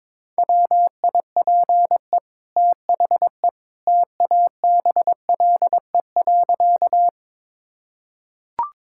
Morse code, heard as a keyed tone, about 22 wpm, 700 Hz. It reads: WIPE THE TABLE.
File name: practice / 22wpm / Easy Sentences with Hints - 22wpm